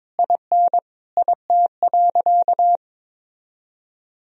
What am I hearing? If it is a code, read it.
Morse code, heard as a keyed tone, about 21 wpm, 700 Hz. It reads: IN IT.
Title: Sentences from Top 1000 Words - Review Part 3 - 22wpm